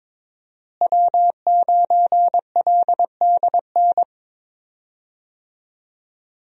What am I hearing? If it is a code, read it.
Morse code, heard as a keyed tone, about 23 wpm, 700 Hz. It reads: W9LDN
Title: USA Call Signs - 5 Characters - Repeated 3x - 22wpm